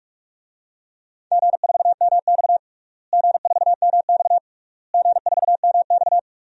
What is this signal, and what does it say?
Morse code, heard as a keyed tone, about 46 wpm, 700 Hz. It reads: G4MX G4MX G4MX
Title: DXCC-100 Call Signs - 4 Characters - Repeated 3x - 45wpm